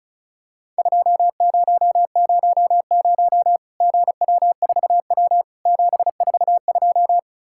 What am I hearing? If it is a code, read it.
Morse code, heard as a keyed tone, about 36 wpm, 700 Hz. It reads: 2000 GW4W 742
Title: Worked All Europe DX Contest - QTC Exchange - 35wpm